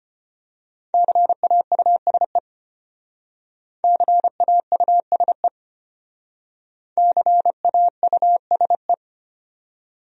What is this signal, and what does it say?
Morse code, heard as a keyed tone, about 29 wpm, 700 Hz. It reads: CAUSE CAUSE CAUSE
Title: Speed-Racing Top 200 Words 25wpm